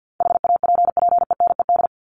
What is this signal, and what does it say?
Morse code, heard as a keyed tone, about 50 wpm, 700 Hz. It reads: HAPPENED
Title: ED Suffix - Top 100 Words with Base Words 50wpm